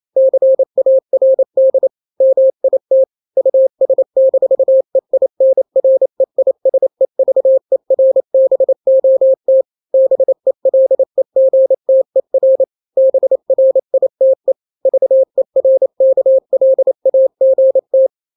Morse code, heard as a keyed tone, about 28 wpm, 530 Hz. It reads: CARD MIT US-EINREISEVERBOT BELEGTER BRITE VERKLAGT
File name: daily / 27wpm